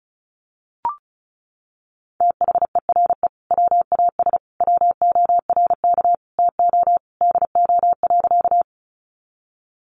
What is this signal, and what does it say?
Morse code, heard as a keyed tone, about 34 wpm, 700 Hz. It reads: THERE WAS WORK TO DO.
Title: Sentences from Top 200 Words 35wpm